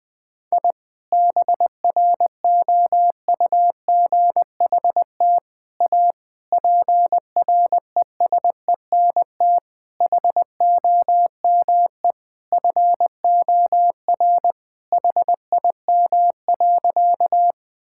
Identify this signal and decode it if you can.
Morse code, heard as a keyed tone, about 21 wpm, 700 Hz. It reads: I BROUGHT A PRESENT HOME FOR HIM.
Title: Sentences from Top 1000 Words - Review Part 5 - 20wpm